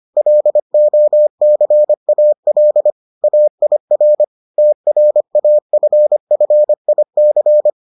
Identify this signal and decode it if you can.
Morse code, heard as a keyed tone, about 25 wpm, 600 Hz. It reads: LOCAL AIR TRAFFIC